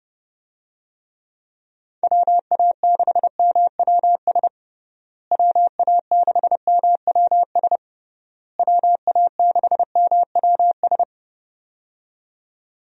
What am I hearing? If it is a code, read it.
Morse code, heard as a keyed tone, about 30 wpm, 700 Hz. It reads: WA6MWS WA6MWS WA6MWS
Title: USA Call Signs - 6 Characters - Repeated 3x - 30wpm